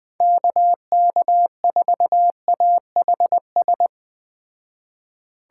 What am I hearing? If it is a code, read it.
Morse code, heard as a keyed tone, about 20 wpm, 700 Hz. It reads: KK4AHS
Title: Speed Racing - USA Call Signs 17wpm